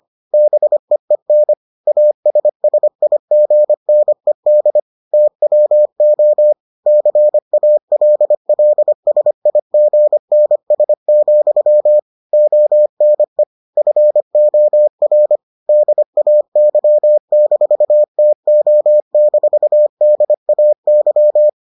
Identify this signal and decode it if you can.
Morse code, heard as a keyed tone, about 25 wpm, 600 Hz. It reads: BEEN ASSIGNED TWO CALLSIGNS, ONE FOR DAY-TO-DAY